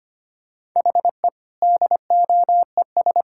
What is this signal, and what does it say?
Morse code, heard as a keyed tone, about 25 wpm, 700 Hz. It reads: HE DOES
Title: Sentences from Top 1000 Words - Review Part 6 - 25wpm